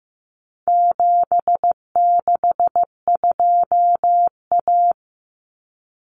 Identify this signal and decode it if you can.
Morse code, heard as a keyed tone, about 15 wpm, 700 Hz. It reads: 762A
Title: Common Cut Numbers - 4-Digit 15wpm